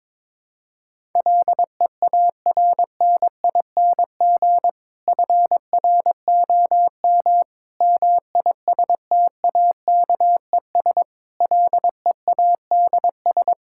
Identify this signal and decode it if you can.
Morse code, heard as a keyed tone, about 22 wpm, 700 Hz. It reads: LEARNING FROM MISTAKES LEADS
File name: Difficult Sentences with Hints - 22wpm